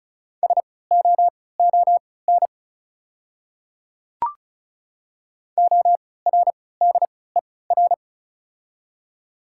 Morse code, heard as a keyed tone, about 34 wpm, spaced as for 20 wpm, 690 Hz. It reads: SOON ORDER
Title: 3x Character Spacing - Top 500 Words - 35wpm